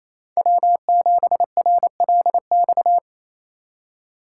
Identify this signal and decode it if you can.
Morse code, heard as a keyed tone, about 36 wpm, 700 Hz. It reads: W7RLX